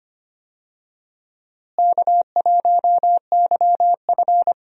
Morse code, heard as a keyed tone, about 25 wpm, 700 Hz. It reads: K1YF
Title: USA Call Signs - 4 Characters - Repeated 3x - 25wpm